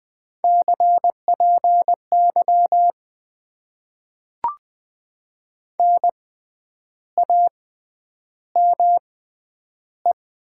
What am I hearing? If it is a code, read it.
Morse code, heard as a keyed tone, about 20 wpm, spaced as for 9 wpm, 700 Hz. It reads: CPY NAME
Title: Top 90 QSO Elements - 6x Character Spacing - 20wpm